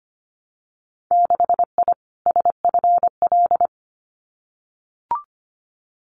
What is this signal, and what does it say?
Morse code, heard as a keyed tone, about 25 wpm, 700 Hz. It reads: 6I SFL
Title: Winter Field Day Exchanges Encore 25wpm